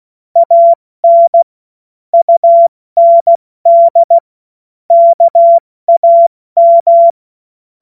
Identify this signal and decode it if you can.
Morse code, heard as a keyed tone, about 16 wpm, 680 Hz. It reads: AN UND KAM
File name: Morsekurs / Witze / Witz 07. März 2023 70 F80